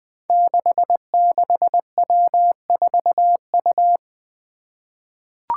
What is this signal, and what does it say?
Morse code, heard as a keyed tone, about 20 wpm, 700 Hz. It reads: 66W4U
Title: Five Challenging Letter-Number Combinations 20wpm